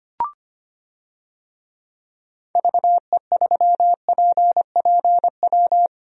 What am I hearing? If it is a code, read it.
Morse code, heard as a keyed tone, about 25 wpm, 700 Hz. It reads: VE3PPW
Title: DXCC-100 Call Signs - 6 Characters - 25wpm